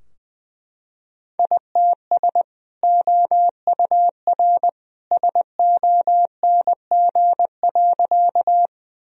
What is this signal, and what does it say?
Morse code, heard as a keyed tone, about 20 wpm, 700 Hz. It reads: ITS OUR SONG.
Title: Sentences from Top 1000 Words - Review Part 2 - 20wpm